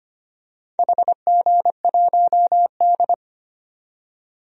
Morse code, heard as a keyed tone, about 25 wpm, 700 Hz. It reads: HG1D
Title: DXCC-100 Call Signs - 4 Characters - 25wpm